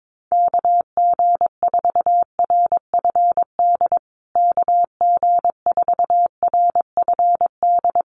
Morse code, heard as a keyed tone, about 22 wpm, 700 Hz. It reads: KG4RFD KG4RFD
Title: USA Call Signs Repeated 3x 22wpm